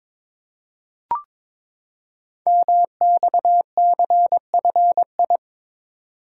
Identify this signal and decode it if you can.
Morse code, heard as a keyed tone, about 22 wpm, 700 Hz. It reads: MXCFI
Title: Code Groups - 5 Letters - 22wpm